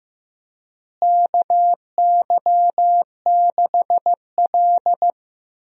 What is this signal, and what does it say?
Morse code, heard as a keyed tone, about 15 wpm, 700 Hz. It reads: KY6L
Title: USA Call Signs - 4 Characters - Repeated 2x - 15wpm